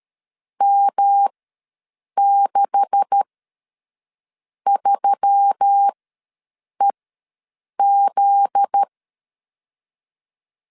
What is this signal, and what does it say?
Morse code, heard as a keyed tone, about 13 wpm, spaced as for 6 wpm, 800 Hz. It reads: M63EZ
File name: LEZIONI / 9 2867 ws 12wpm 3431